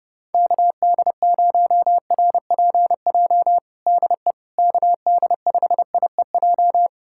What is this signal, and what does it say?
Morse code, heard as a keyed tone, about 30 wpm, 700 Hz. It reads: KD0RPJ DE KD5IEJ
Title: USA Callsign DE Callsign - 6 Characters - 30wpm